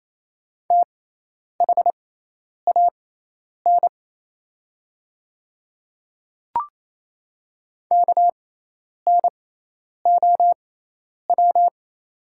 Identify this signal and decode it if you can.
Morse code, heard as a keyed tone, about 28 wpm, spaced as for 10 wpm, 690 Hz. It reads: THAN KNOW
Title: 6x Character Spacing - Top 500 Words - 28wpm